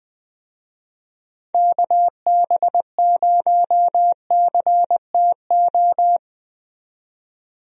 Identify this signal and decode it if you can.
Morse code, heard as a keyed tone, about 21 wpm, 700 Hz. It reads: KB0CTO
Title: USA Call Signs - 6 Characters - 20wpm